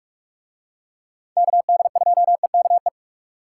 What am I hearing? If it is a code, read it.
Morse code, heard as a keyed tone, about 46 wpm, 700 Hz. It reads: KD2EKE